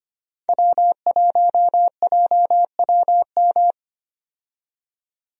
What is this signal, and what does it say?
Morse code, heard as a keyed tone, about 25 wpm, 700 Hz. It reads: W1JWM
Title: USA Call Signs - 5 Characters - 25wpm